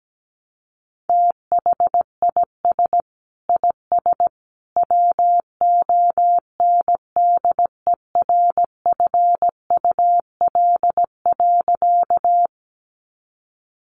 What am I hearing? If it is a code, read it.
Morse code, heard as a keyed tone, about 17 wpm, 700 Hz. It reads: THIS IS WONDERFUL.
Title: Sentences from Top 1100 Words 17wpm